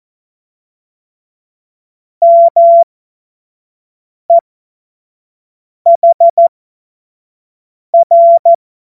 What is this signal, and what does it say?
Morse code, heard as a keyed tone, about 14 wpm, spaced as for 5 wpm, 680 Hz. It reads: MEHR
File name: Morsekurs / Witze / Witz 13. Juli 2021, 25 F70